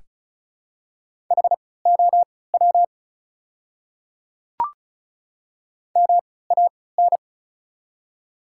Morse code, heard as a keyed tone, about 34 wpm, spaced as for 20 wpm, 690 Hz. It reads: HOW MAN
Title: 3x Character Spacing - Top 300 Words - 35wpm